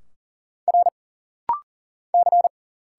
Morse code, heard as a keyed tone, about 40 wpm, 700 Hz. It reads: R C
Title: Single Letters - Rapid-Fire 40wpm